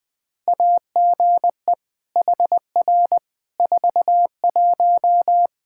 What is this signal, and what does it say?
Morse code, heard as a keyed tone, about 20 wpm, 700 Hz. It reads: AGE HR 41
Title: Practice QSO - Exchange 3 - More Info - 20wpm